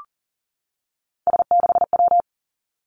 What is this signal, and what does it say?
Morse code, heard as a keyed tone, about 38 wpm, 700 Hz. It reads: S6W
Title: Three Challenging Letter-Number Combinations 40wpm